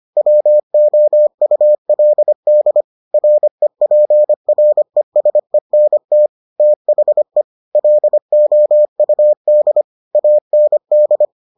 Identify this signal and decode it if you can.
Morse code, heard as a keyed tone, about 25 wpm, 600 Hz. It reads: WOULD REPRESENT THE LOUD AND